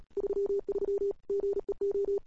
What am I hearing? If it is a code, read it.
Morse code, heard as a keyed tone, about 34 wpm, 410 Hz. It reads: 33 GEO
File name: CWSprint / Feb2007 / 2007-feb-cw-sprint-k5tr-op-n5rz -00:00